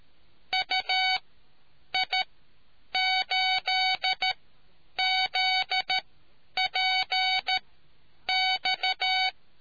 Morse code, heard as a keyed tone, about 13 wpm, spaced as for 8 wpm, 750 Hz. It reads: UI8ZPX